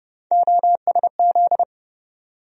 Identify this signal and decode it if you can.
Morse code, heard as a keyed tone, about 30 wpm, 700 Hz. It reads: OSZ
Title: USA Call Sign Postfixes - 3 Letters 30wpm